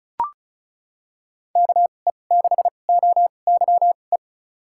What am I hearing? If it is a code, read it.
Morse code, heard as a keyed tone, about 34 wpm, spaced as for 25 wpm, 700 Hz. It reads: KE6OYE
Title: USA Call Signs - 2x Character Spacing - 35wpm